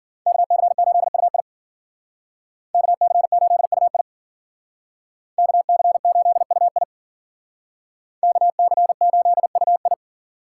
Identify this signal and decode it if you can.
Morse code, heard as a keyed tone, about 52 wpm, 700 Hz. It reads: KC8UI KC8UI KC8UI KC8UI